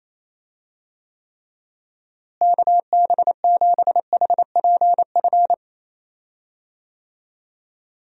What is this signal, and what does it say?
Morse code, heard as a keyed tone, about 28 wpm, 700 Hz. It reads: KB7HPF
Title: USA Call Signs - 6 Characters - Repeated 2x - 28wpm